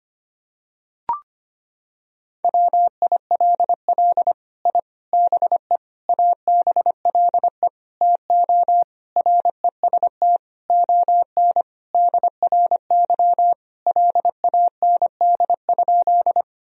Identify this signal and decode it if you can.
Morse code, heard as a keyed tone, about 25 wpm, 700 Hz. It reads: WILL I BE ABLE TO REST ON DRY LAND?
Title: Sentences from Top 1000 Words - Review Part 3 - 25wpm